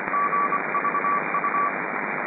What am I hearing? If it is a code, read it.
Morse code, heard as a keyed tone, about 24 wpm, 1150 Hz, with radio noise. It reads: GVU